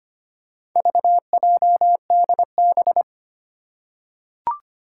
Morse code, heard as a keyed tone, about 25 wpm, 700 Hz. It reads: VJDB
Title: Four Challenging Letter-Number Combinations 25wpm